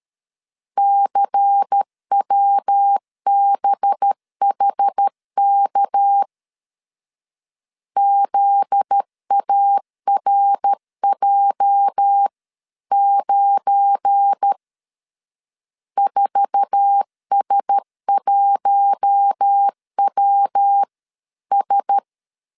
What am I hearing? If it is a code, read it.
Morse code, heard as a keyed tone, about 13 wpm, 800 Hz. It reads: CWBHK ZARJ9 4S1WS
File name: CW / DET / 1 9 1500C S A 12wpm 3800